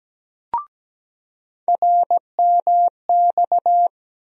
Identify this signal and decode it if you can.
Morse code, heard as a keyed tone, about 17 wpm, 700 Hz. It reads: RMX
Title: USA Call Sign Postfixes - 3 Letters 17wpm